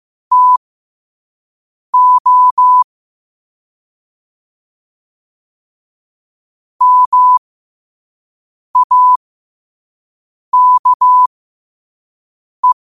Morse code, heard as a keyed tone, about 15 wpm, spaced as for 5 wpm, 1000 Hz. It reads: TO MAKE
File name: qotdfiles / 5wpm